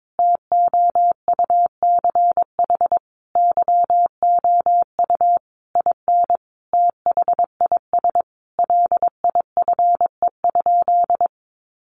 Morse code, heard as a keyed tone, about 22 wpm, 700 Hz. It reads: TOUCH YOU IN THIS LIFE?
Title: Sentences from Top 1200 Words 22wpm